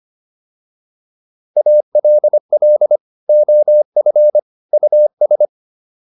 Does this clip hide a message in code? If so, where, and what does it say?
Yes: Morse code from 1.6 to 5.5 s, heard as a keyed tone, about 25 wpm, 600 Hz. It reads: ALL OF US